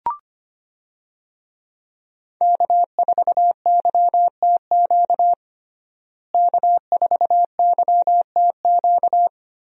Morse code, heard as a keyed tone, about 25 wpm, 700 Hz. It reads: K4YTQ K4YTQ
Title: USA Call Signs - 5 Characters - Repeated 3x - 25wpm